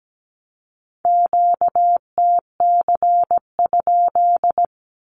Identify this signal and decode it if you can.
Morse code, heard as a keyed tone, about 17 wpm, 700 Hz. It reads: QTC?
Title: Q-Codes with Spoken Meaning - 17wpm